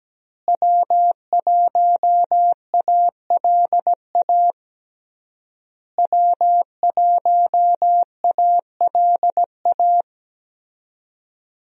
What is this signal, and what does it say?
Morse code, heard as a keyed tone, about 17 wpm, 700 Hz. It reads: W1ALA W1ALA
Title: USA Call Signs - 5 Characters - Repeated 3x - 17wpm